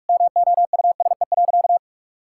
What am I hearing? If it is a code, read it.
Morse code, heard as a keyed tone, about 43 wpm, 700 Hz. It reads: MOUSE.
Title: Sentences from Top 1000 Words - Review Part 5 - 45wpm